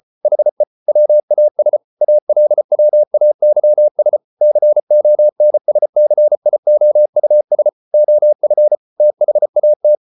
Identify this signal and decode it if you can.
Morse code, heard as a keyed tone, about 34 wpm, 600 Hz. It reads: HE WAS ALWAYS CONSCIOUS OF THAT